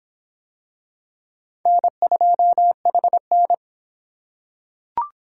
Morse code, heard as a keyed tone, about 22 wpm, 700 Hz. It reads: N2HN